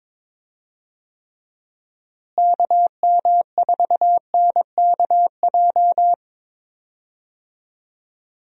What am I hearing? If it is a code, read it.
Morse code, heard as a keyed tone, about 22 wpm, 700 Hz. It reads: KM4NKJ
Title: USA Call Signs - 6 Characters - Repeated 2x - 22wpm